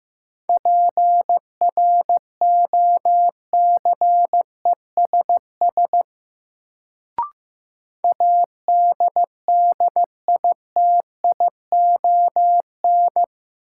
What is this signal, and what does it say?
Morse code, heard as a keyed tone, about 15 wpm, 700 Hz. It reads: PROCESS ADDITION
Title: Sets of 3 Words - New Word Drill - 15wpm